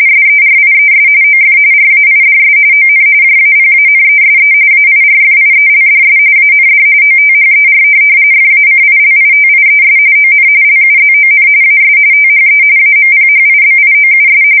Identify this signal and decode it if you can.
RTTY, about 45.45 baud. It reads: SERIES OF RAIDS ON JAPANESE AIRFIELDS AND OTHER FACILITIES ON LUZON AS THE AMPHIBIOUS